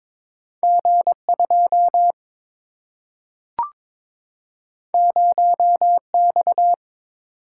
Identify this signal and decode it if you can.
Morse code, heard as a keyed tone, about 22 wpm, 700 Hz. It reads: G2 0X
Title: Single-Letter Single-Number 22wpm